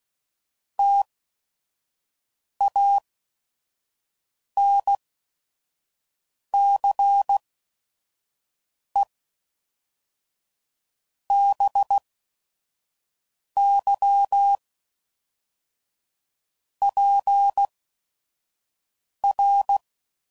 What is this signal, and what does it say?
Morse code, heard as a keyed tone, about 16 wpm, 800 Hz. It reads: T A N C E B Y P R